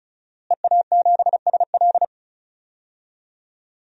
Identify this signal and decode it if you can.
Morse code, heard as a keyed tone, about 36 wpm, 700 Hz. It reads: EA7SL
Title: DXCC-100 Call Signs - 5 Characters - Repeated 3x - 35wpm